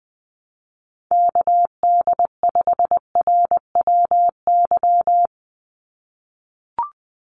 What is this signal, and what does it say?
Morse code, heard as a keyed tone, about 20 wpm, 700 Hz. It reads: KD5RWY
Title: USA Call Signs 20wpm